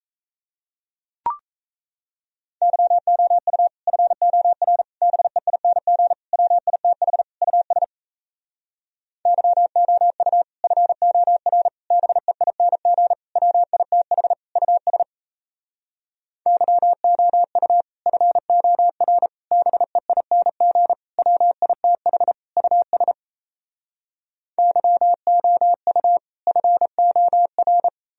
Morse code, heard as a keyed tone, about 37 wpm, 700 Hz. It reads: YOU FOR BEING WITH US YOU FOR BEING WITH US YOU FOR BEING WITH US YOU FOR